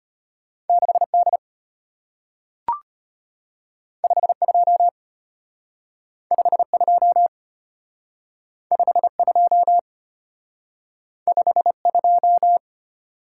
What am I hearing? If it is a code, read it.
Morse code, heard as a keyed tone, about 33 wpm, 690 Hz. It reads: 6D 52 52 52 52